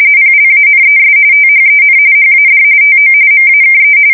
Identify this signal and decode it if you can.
RTTY, about 45.45 baud. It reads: PLANES THEN HAD TO MAKE
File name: fsk170 / battleshipnj